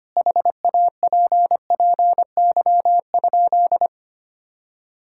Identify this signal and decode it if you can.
Morse code, heard as a keyed tone, about 25 wpm, 700 Hz. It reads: HAPPY?
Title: Sentences from Top 1000 Words - Review Part 6 - 25wpm